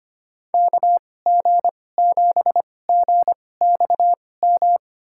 Morse code, heard as a keyed tone, about 25 wpm, spaced as for 18 wpm, 700 Hz. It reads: KG7GXM
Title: USA Call Signs - 6 Characters - 2x Character Spacing - 25wpm